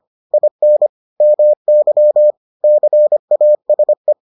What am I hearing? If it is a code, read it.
Morse code, heard as a keyed tone, about 25 wpm, 600 Hz. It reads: IN MY CASE